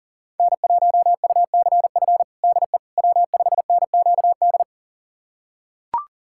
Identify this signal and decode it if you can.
Morse code, heard as a keyed tone, about 40 wpm, 700 Hz. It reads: N1UCF DE W5NQD
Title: USA Callsign DE Callsign - 5 Characters - 40wpm